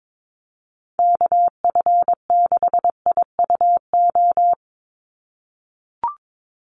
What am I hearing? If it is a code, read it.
Morse code, heard as a keyed tone, about 22 wpm, 700 Hz. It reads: KF6IUO
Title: USA Call Signs 22wpm